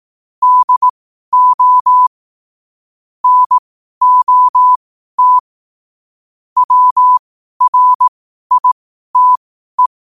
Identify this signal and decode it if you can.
Morse code, heard as a keyed tone, about 18 wpm, spaced as for 12 wpm, 1000 Hz. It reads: DO NOT WRITE
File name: qotdfiles / 12wpm